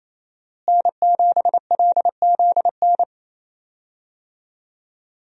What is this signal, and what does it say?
Morse code, heard as a keyed tone, about 28 wpm, 700 Hz. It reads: N7LZN
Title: USA Call Signs - 5 Characters - Repeated 2x - 28wpm